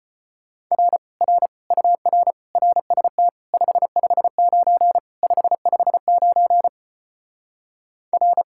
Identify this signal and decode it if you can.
Morse code, heard as a keyed tone, about 33 wpm, 700 Hz. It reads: R R UR RST 559 559 R